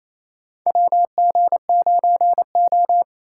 Morse code, heard as a keyed tone, about 28 wpm, 700 Hz. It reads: WG9O
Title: USA Call Signs - 3x Character Spacing - 28wpm